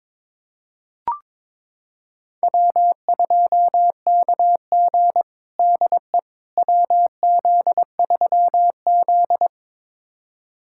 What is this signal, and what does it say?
Morse code, heard as a keyed tone, about 22 wpm, 700 Hz. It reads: W2KG DE WZ3Z
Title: USA Callsign DE Callsign - 4 Characters - 22wpm